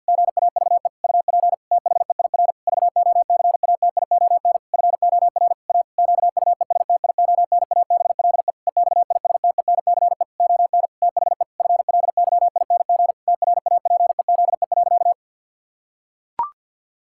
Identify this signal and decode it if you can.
Morse code, heard as a keyed tone, about 52 wpm, 700 Hz. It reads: GAVE UP THEIR VOCATION FOR A QUESTIONABLE EXISTENCE ON THE FLYING TRAPEZE.
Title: Pangram Challenge v2 - 50wpm